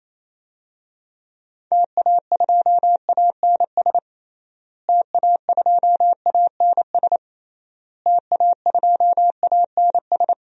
Morse code, heard as a keyed tone, about 27 wpm, 700 Hz. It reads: TA2ANS TA2ANS TA2ANS
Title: DXCC-100 Call Signs - 6 Characters - Repeated 3x - 28wpm